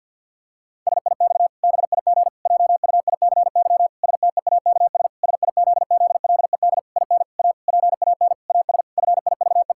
Morse code, heard as a keyed tone, about 50 wpm, 700 Hz. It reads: SIX BIG JUICY STEAKS SIZZLED IN A PAN AS FIVE
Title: Pangram Challenge v2 - 50wpm